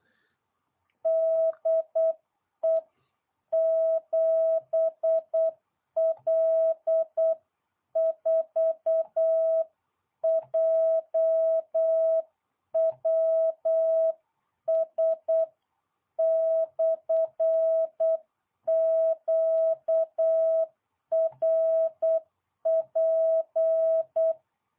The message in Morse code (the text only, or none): DE 7L4JWS/QRP